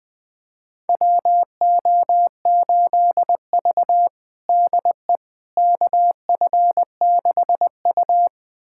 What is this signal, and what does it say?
Morse code, heard as a keyed tone, about 21 wpm, 700 Hz. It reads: WO8V DE KF6U
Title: USA Callsign DE Callsign - 4 Characters - 20wpm